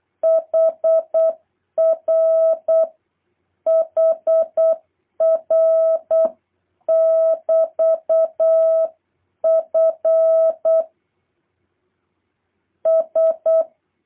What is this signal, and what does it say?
Morse code, heard as a keyed tone, about 8 wpm, 660 Hz. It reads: HR HR <BT> F S
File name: jh8cbh / 2023cw